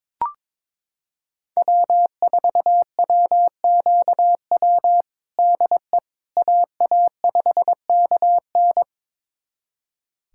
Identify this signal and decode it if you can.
Morse code, heard as a keyed tone, about 22 wpm, 700 Hz. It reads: W4WQW DE AA5KN
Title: USA Callsign DE Callsign - 5 Characters - 22wpm